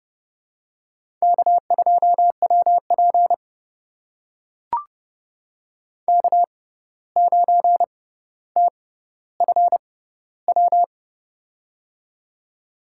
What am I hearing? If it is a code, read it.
Morse code, heard as a keyed tone, about 31 wpm, spaced as for 15 wpm, 700 Hz. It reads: K2WP K9TFW
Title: USA Call Signs - 6x Character Spacing - 30wpm